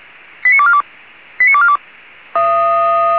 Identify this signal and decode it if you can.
Five-tone selective call (ZVEI), heard as quick runs of five tones, tones of about 70 ms. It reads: 89242 89242